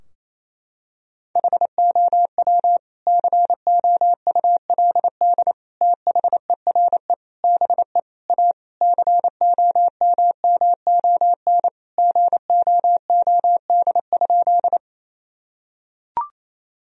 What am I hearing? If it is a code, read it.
Morse code, heard as a keyed tone, about 28 wpm, 700 Hz. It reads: HOW COULD THERE BE A COMMON GOOD?
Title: Sentences from Top 1000 Words - Review Part 3 - 28wpm